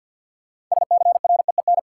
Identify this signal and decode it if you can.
Morse code, heard as a keyed tone, about 52 wpm, 700 Hz. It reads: SCREEN
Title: Sentences from Top 1000 Words - Review Part 6 - New Word Drill - 50wpm